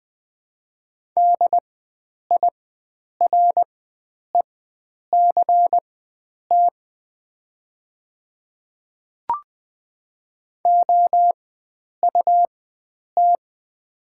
Morse code, heard as a keyed tone, about 20 wpm, spaced as for 9 wpm, 700 Hz. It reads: DIRECT OUT
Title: 4x Character Spacing - Top 500 Words - 20wpm